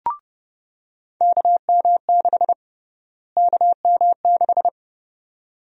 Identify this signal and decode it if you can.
Morse code, heard as a keyed tone, about 30 wpm, 700 Hz. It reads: KM6 KM6
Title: USA Call Sign Prefixes - Repeated 2x - 30wpm